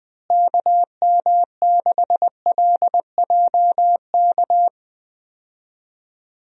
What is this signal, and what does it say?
Morse code, heard as a keyed tone, about 21 wpm, 700 Hz. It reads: KM6LJK
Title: USA Call Signs - 6 Characters - Repeated 2x - 20wpm